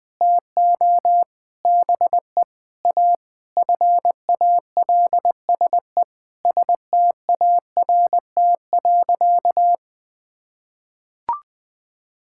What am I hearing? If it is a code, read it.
Morse code, heard as a keyed tone, about 19 wpm, 700 Hz. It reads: TO BE A FALSE START.
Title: Sentences from Top 1600 Words 20wpm